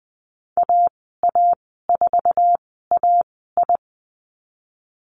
Morse code, heard as a keyed tone, about 20 wpm, spaced as for 14 wpm, 700 Hz. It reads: AA4AI
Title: 076 Lesson 13g - Letter L - Call Signs 20wpm